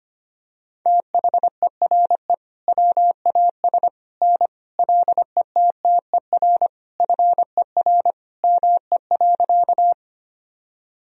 Morse code, heard as a keyed tone, about 25 wpm, 700 Hz. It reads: THERE WAS N LETTER FER ME.
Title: Sentences from Top 300 Words and Abbreviations 25wpm